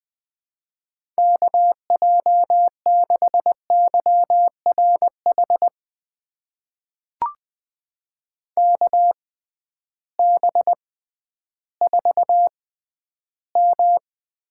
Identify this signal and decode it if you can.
Morse code, heard as a keyed tone, about 21 wpm, spaced as for 12 wpm, 700 Hz. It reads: KJ6YRH KB4M
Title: USA Call Signs - 6 Characters - 6x Character Spacing - 20wpm